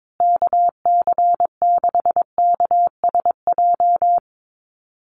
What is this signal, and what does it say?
Morse code, heard as a keyed tone, about 22 wpm, 700 Hz. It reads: KC6KSJ
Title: USA Call Signs Repeated 3x 22wpm